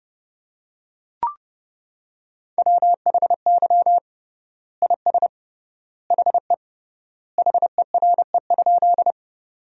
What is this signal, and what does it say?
Morse code, heard as a keyed tone, about 32 wpm, 700 Hz. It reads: WHY IS HE HERE?
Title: 3x Word-Spacing - Sentences from Top 200 Words - 30wpm